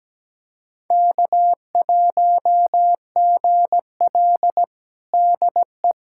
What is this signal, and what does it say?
Morse code, heard as a keyed tone, about 17 wpm, 700 Hz. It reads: K1GL DE
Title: USA Callsign DE Callsign - 4 Characters - 17wpm